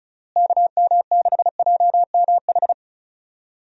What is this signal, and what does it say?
Morse code, heard as a keyed tone, about 34 wpm, 700 Hz. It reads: KM6JMH
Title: USA Call Signs - 6 Characters - Repeated 3x - 35wpm